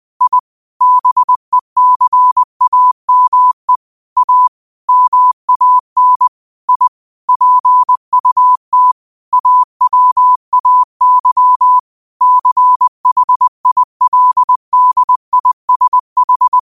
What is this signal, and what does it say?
Morse code, heard as a keyed tone, about 20 wpm, 1000 Hz. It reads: I BECAME A MAN I PUT AWAY CHILDISH
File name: 20wpm